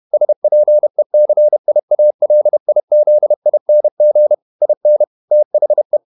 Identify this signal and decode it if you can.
Morse code, heard as a keyed tone, about 31 wpm, 600 Hz. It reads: SPECIALIZING IN THE
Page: Never 31wpm | CW med Gnister